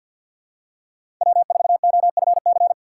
Morse code, heard as a keyed tone, about 47 wpm, 700 Hz. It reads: W4OFC
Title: USA Call Signs - 2x Character Spacing - 50wpm